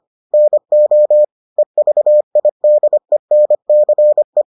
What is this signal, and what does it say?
Morse code, heard as a keyed tone, about 25 wpm, 600 Hz. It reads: NO EVIDENCE